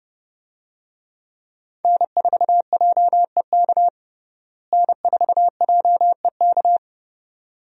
Morse code, heard as a keyed tone, about 32 wpm, 700 Hz. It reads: N4JEK N4JEK
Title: USA Call Signs - 5 Characters - Repeated 2x - 30wpm